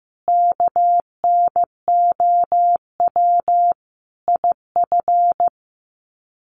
Morse code, heard as a keyed tone, about 15 wpm, 700 Hz. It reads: KNOW IF
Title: Sets of 5 Words Encore 15wpm